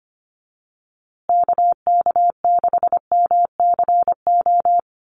Morse code, heard as a keyed tone, about 25 wpm, 700 Hz. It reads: KK6MCO
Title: USA Call Signs Repeated 3x 25wpm